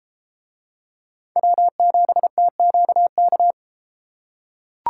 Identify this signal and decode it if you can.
Morse code, heard as a keyed tone, about 30 wpm, 700 Hz. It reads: W7TQK